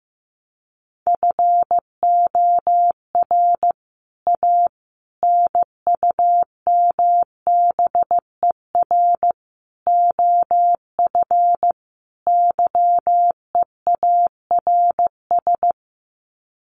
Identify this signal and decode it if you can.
Morse code, heard as a keyed tone, about 15 wpm, 700 Hz. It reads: FOR A NUMBER OF YEARS
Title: Sets of 5 Words Encore 15wpm